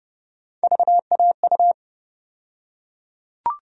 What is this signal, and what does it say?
Morse code, heard as a keyed tone, about 30 wpm, 700 Hz. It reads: VAU
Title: USA Call Sign Postfixes - 3 Letters 30wpm